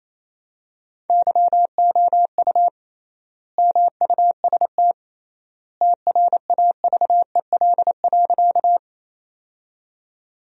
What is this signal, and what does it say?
Morse code, heard as a keyed tone, about 28 wpm, 700 Hz. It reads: YOU MUST TRAVEL.
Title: 3x Word-Spacing - Sentences from Top 400 Words - 28wpm